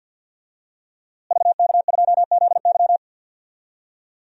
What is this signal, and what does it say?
Morse code, heard as a keyed tone, about 50 wpm, 700 Hz. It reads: VK2ZY
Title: DXCC-100 Call Signs - 5 Characters - Repeated 2x - 50wpm